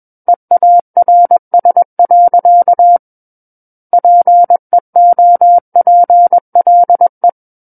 Morse code, heard as a keyed tone, about 21 wpm, 700 Hz. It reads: EARS. PEOPLE
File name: Book One / 12-Chapter 12 - 21 WPM